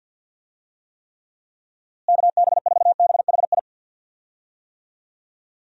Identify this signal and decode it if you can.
Morse code, heard as a keyed tone, about 46 wpm, 700 Hz. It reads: KB4BHI
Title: USA Call Signs - 6 Characters - Repeated 2x - 50wpm